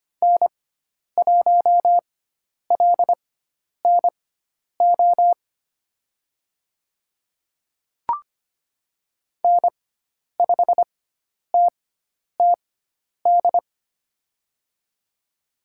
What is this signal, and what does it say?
Morse code, heard as a keyed tone, about 25 wpm, spaced as for 10 wpm, 700 Hz. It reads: N1LNO N5TTD
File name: Lesson 16g - Letter c - Call Signs 25wpm